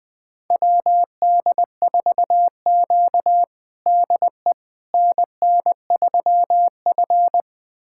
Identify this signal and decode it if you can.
Morse code, heard as a keyed tone, about 21 wpm, 700 Hz. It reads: WD4Q DE NN3F
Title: USA Callsign DE Callsign - 4 Characters - 20wpm